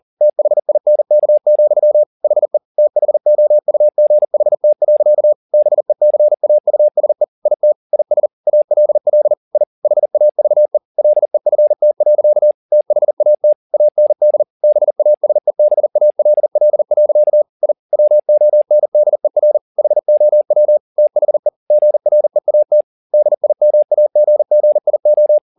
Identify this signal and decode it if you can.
Morse code, heard as a keyed tone, about 40 wpm, 600 Hz. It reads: THINK, HE THOUGHT. BECAUSE IT IS ALL I HAVE LEFT. THAT AND BASEBALL. I WONDER HOW THE GREAT DIMAGGIO
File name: the old man and the sea 2 40wpm